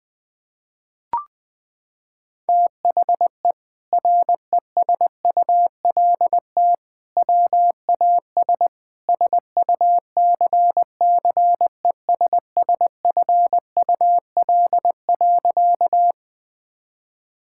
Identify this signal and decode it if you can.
Morse code, heard as a keyed tone, about 20 wpm, 700 Hz. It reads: THE RESULT WAS SUCCESSFUL.
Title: Sentences from Top 1000 Words - Review Part 6 - 20wpm